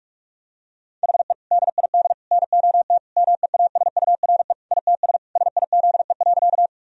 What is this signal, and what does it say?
Morse code, heard as a keyed tone, about 41 wpm, 700 Hz. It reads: HE DID NOT MEASURE ITS SIZE.
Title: Sentences from Top 1000 Words - Review Part 2 - 45wpm